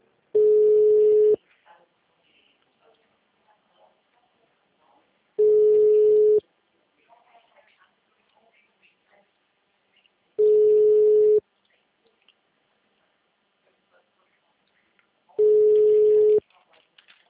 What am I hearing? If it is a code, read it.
Ringback tone (Europe, 425 Hz), with speech.
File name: out- -2009- - - .73031